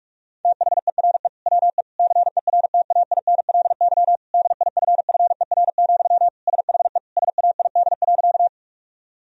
Morse code, heard as a keyed tone, about 47 wpm, 700 Hz. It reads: THERE WE CERTAINLY DIFFER, SHE SAID.
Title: Sentences from Top 1000 Words - Review Part 1 - 45wpm